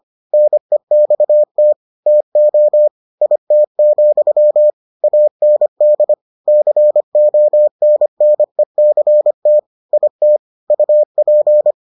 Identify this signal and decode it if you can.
Morse code, heard as a keyed tone, about 25 wpm, 600 Hz. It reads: NEXT TO IT, AND CONNECT IT UP